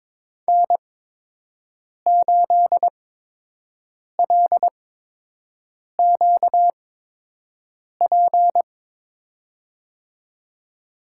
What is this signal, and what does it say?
Morse code, heard as a keyed tone, about 23 wpm, spaced as for 6 wpm, 700 Hz. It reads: N8LQP
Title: USA Call Signs - 5 Characters - 8x Character Spacing - 22wpm